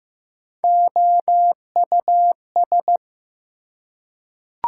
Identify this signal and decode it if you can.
Morse code, heard as a keyed tone, about 15 wpm, 700 Hz. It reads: OUS
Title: Prefixes and Suffixes 15wpm